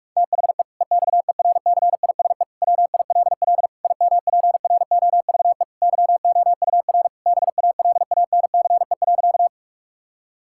Morse code, heard as a keyed tone, about 46 wpm, 700 Hz. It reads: THE EXERCISE WILL IMPROVE YOUR BALANCE.
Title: Sentences from Top 1000 Words - Review Part 6 - 45wpm